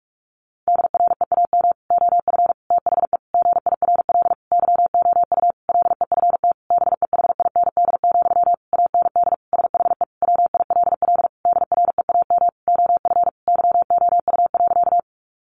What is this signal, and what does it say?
Morse code, heard as a keyed tone, about 44 wpm, 700 Hz. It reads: DREAM OF THE GIRL YOU LEFT BEHIND, AND SHE WILL DREAM OF YOU.
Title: Sentences from Top 1500 Words 45wpm